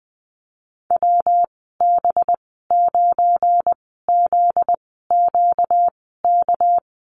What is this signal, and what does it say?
Morse code, heard as a keyed tone, about 20 wpm, spaced as for 14 wpm, 700 Hz. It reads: WB9ZQK
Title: 245 Lesson 39g - Letter z - Call Signs 20wpm